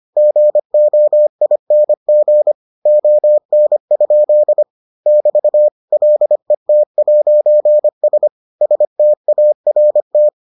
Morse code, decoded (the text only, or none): GOING ON? <BT> LET'S START